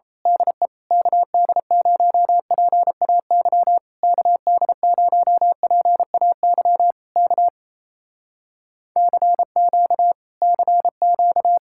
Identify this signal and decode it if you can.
Morse code, heard as a keyed tone, about 27 wpm, 700 Hz. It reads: DE KD0PAY KD0PAY K CQ CQ
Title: Speed-Racing Common QSO Phrases - 28wpm